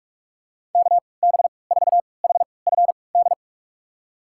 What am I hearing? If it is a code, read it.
Morse code, heard as a keyed tone, about 42 wpm, spaced as for 25 wpm, 700 Hz. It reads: KB4HFD
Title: USA Call Signs - 6 Characters - 3x Character Spacing - 45wpm